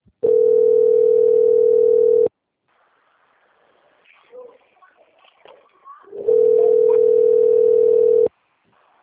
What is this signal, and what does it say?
Ringback tone (North America), 440 + 480 Hz.